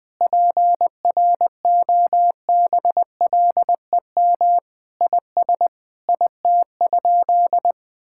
Morse code, heard as a keyed tone, about 20 wpm, 700 Hz. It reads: PROBLEM IS IT?
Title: Sentences from Top 1000 Words - Review Part 6 - 20wpm